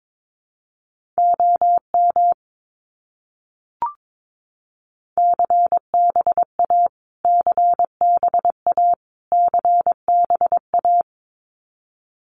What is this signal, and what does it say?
Morse code, heard as a keyed tone, about 22 wpm, 700 Hz. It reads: OM CBA CBA CBA
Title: Abbreviations Repeated 3x - 22wpm